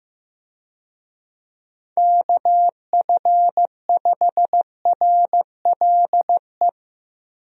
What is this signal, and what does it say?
Morse code, heard as a keyed tone, about 15 wpm, 700 Hz. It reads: KF5RLE
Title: USA Call Signs - 6 Characters - Repeated 3x - 15wpm